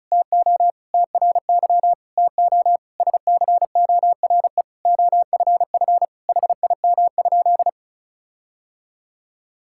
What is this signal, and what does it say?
Morse code, heard as a keyed tone, about 36 wpm, 700 Hz. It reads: TO TRY TO SCORE OFF HIM?
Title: Sentences from Top 1600 Words 35wpm